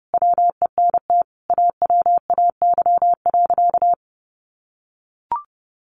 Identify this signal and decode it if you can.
Morse code, heard as a keyed tone, about 30 wpm, 700 Hz. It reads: WENT AWAY.
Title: Sentences from Top 600 Words 30wpm